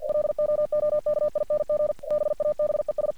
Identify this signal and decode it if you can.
Morse code, heard as a keyed tone, about 50 wpm, 600 Hz. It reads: LOOKING BABES
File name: archived / CRSnet